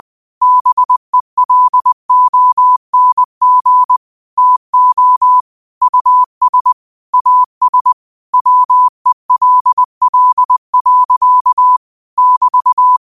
Morse code, heard as a keyed tone, about 20 wpm, 1000 Hz. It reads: BELONG TO US AS WELL. <BT>
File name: qotdfiles / 20wpm